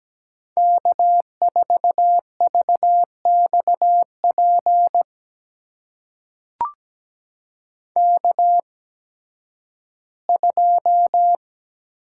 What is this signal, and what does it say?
Morse code, heard as a keyed tone, about 17 wpm, spaced as for 11 wpm, 700 Hz. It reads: K4VXP K2
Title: USA Call Signs - 5 Characters - 8x Character Spacing - 17wpm